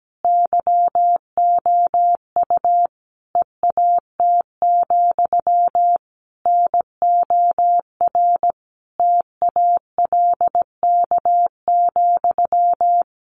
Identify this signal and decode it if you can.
Morse code, heard as a keyed tone, about 17 wpm, 700 Hz. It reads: YOU EAT, NOR TALK,
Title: Sentences from Top 1500 Words 17wpm